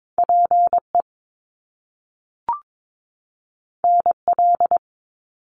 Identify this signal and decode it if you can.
Morse code, heard as a keyed tone, about 22 wpm, 700 Hz. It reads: PE NL
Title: Canadian Provinces and Territories 22wpm